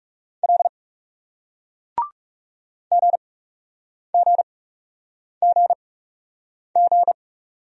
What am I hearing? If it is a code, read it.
Morse code, heard as a keyed tone, about 39 wpm, spaced as for 8 wpm, 700 Hz. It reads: L GGGG